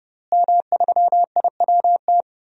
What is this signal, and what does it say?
Morse code, heard as a keyed tone, about 30 wpm, 700 Hz. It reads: M3IWT
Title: DXCC-100 Call Signs - 5 Characters - Repeated 3x - 30wpm